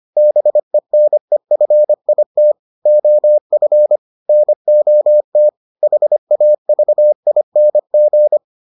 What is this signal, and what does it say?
Morse code, heard as a keyed tone, about 25 wpm, 600 Hz. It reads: BENEFIT OF NOT HAVING